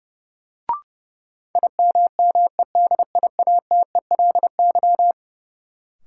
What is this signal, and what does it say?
Morse code, heard as a keyed tone, about 30 wpm, 700 Hz. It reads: IMMEDIATELY
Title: Sentences from Top 1100 Words - New Word Drill Alt - 30wpm